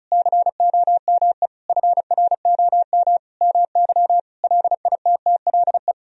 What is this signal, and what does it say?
Morse code, heard as a keyed tone, about 34 wpm, 700 Hz. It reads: COME FROM MY LITTLE
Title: Sentences from Top 1000 Words - Review Part 4 - 35wpm